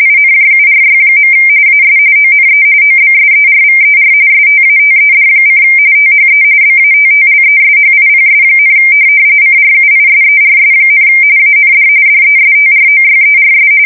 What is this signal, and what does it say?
RTTY, about 45.45 baud. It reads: INJURY. AS ONE OBSERVER STATED: THE BATTLESHIPS, CRUISERS, AND DESTROYERS
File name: fsk170 / USS-New-Jersey